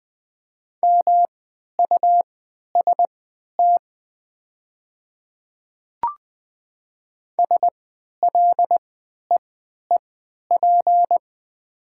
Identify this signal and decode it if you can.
Morse code, heard as a keyed tone, about 20 wpm, spaced as for 11 wpm, 700 Hz. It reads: MUST SLEEP
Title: 3x Character Spacing - Top 500 Words - 20wpm